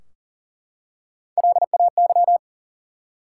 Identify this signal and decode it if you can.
Morse code, heard as a keyed tone, about 42 wpm, 700 Hz. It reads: LAY
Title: Three Letter Words - 40wpm